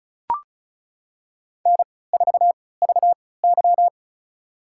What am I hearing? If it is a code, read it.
Morse code, heard as a keyed tone, about 34 wpm, spaced as for 20 wpm, 700 Hz. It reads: N4VY
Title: USA Call Signs - 4 Characters - 3x Character Spacing - 35wpm